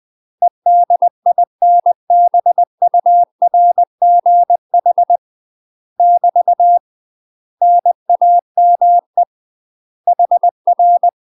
Morse code, heard as a keyed tone, about 20 wpm, 700 Hz. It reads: EDINBURGH <BT> NAME HR
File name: UK MASTER QSOs TRAINING / 20WPM / UK MASTER QSOs 0063